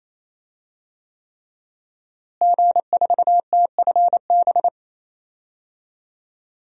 Morse code, heard as a keyed tone, about 28 wpm, 700 Hz. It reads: G4TFB
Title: DXCC-100 Call Signs - 5 Characters - Repeated 3x - 28wpm